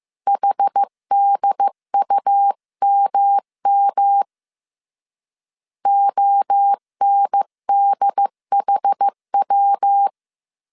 Morse code, heard as a keyed tone, about 15 wpm, 800 Hz. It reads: HDUMM ONDHW
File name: CW / DET / 1 3 1500C S A 14wpm 2447